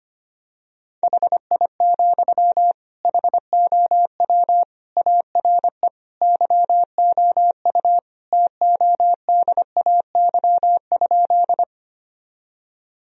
Morse code, heard as a keyed tone, about 25 wpm, 700 Hz. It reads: HI, HOW ARE YOU TODAY?
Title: Sentences from Top 1000 Words - Review Part 3 - 25wpm